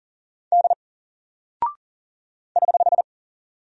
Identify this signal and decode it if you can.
Morse code, heard as a keyed tone, about 39 wpm, 700 Hz. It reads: D <HH>
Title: Single Character (Letter:Number:Punctuation:Prosign) 40wpm